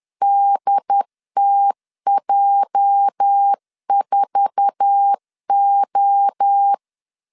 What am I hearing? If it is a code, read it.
Morse code, heard as a keyed tone, about 10 wpm, 800 Hz. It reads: DTJ4O
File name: LEZIONI / 9 2 10wpm 2736